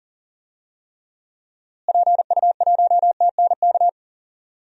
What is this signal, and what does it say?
Morse code, heard as a keyed tone, about 38 wpm, 700 Hz. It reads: PU1TNK
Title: DXCC-100 Call Signs - 6 Characters - Repeated 3x - 40wpm